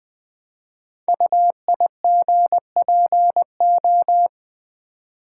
Morse code, heard as a keyed tone, about 20 wpm, 700 Hz. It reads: UIGPO
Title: Code Groups - 5 Letters - 20wpm